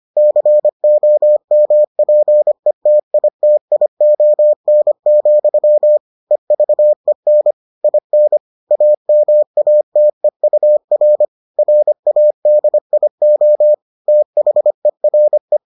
Morse code, heard as a keyed tone, about 25 wpm, 600 Hz. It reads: COMPETITION, EVEN IN AMATEUR RADIO THERE